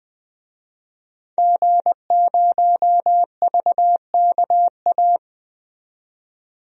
Morse code, heard as a keyed tone, about 20 wpm, 700 Hz. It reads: G0VKA
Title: DXCC-100 Call Signs - 5 Characters - 20wpm